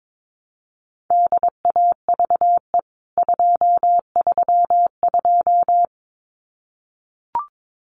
Morse code, heard as a keyed tone, about 22 wpm, 700 Hz. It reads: DAVE 232
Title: CWT Exchange 22wpm